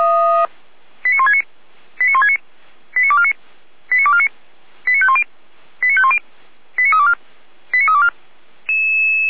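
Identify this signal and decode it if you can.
Five-tone selective call (ZVEI), heard as quick runs of five tones, tones of about 70 ms. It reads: 89169 89169 89249 89249 89510 89510 89324 89324
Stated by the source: ZVEI five-tone call received on SDR for Feuerwehr Orlamünde.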